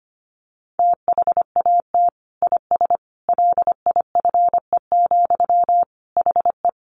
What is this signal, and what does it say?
Morse code, heard as a keyed tone, about 25 wpm, 700 Hz. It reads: THAT IS LIFE, HE
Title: Sentences from Top 900 Words 25wpm